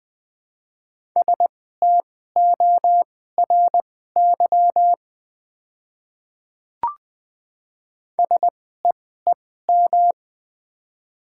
Morse code, heard as a keyed tone, about 20 wpm, spaced as for 14 wpm, 700 Hz. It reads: STORY SEEM
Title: 2x Character Spacing - Top 500 Words - 20wpm